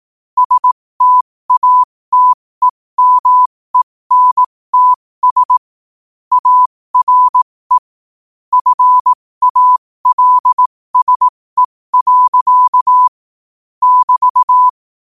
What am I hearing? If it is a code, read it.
Morse code, heard as a keyed tone, about 18 wpm, spaced as for 15 wpm, 1000 Hz. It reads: STATEMENTS ARE FALSE. <BT>
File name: qotdfiles / 15wpm